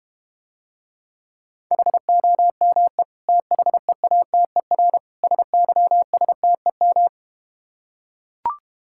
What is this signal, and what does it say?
Morse code, heard as a keyed tone, about 32 wpm, 690 Hz. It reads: HOME THEATER SYSTEM